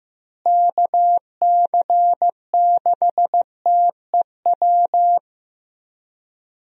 Morse code, heard as a keyed tone, about 15 wpm, 700 Hz. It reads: KC6TEW
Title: USA Call Signs - 6 Characters - Repeated 3x - 15wpm